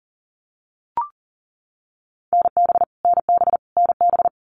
Morse code, heard as a keyed tone, about 40 wpm, 700 Hz. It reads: NB NB NB
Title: Canadian Provinces and Territories - Repeated 3x - 40wpm